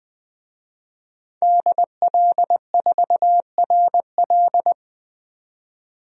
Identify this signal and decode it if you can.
Morse code, heard as a keyed tone, about 20 wpm, 700 Hz. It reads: DL4RL
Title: DXCC-100 Call Signs - 5 Characters - 20wpm